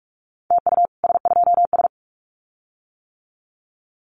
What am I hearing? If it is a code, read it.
Morse code, heard as a keyed tone, about 46 wpm, 700 Hz. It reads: TU S2S
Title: Common QSO Phrases Encore 45wpm